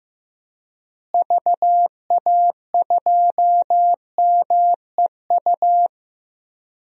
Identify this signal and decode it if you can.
Morse code, heard as a keyed tone, about 15 wpm, 700 Hz. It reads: VA2MEU
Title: DXCC-100 Call Signs - 6 Characters - Repeated 3x - 15wpm